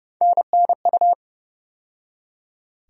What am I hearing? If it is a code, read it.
Morse code, heard as a keyed tone, about 29 wpm, 700 Hz. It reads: NNU
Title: USA Call Sign Postfixes - 3 Letters 30wpm